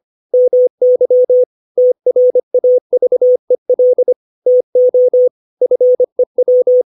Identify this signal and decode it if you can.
Morse code, heard as a keyed tone, about 25 wpm, 500 Hz. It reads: MY TRAVEL TO FEW
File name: 632 words / 632 Words 25WPM